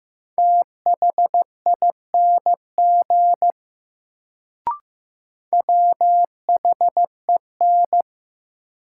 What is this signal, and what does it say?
Morse code, heard as a keyed tone, about 15 wpm, 700 Hz. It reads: THING WHEN
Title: Sentences from Top 100 Words - New Word Drill - 15wpm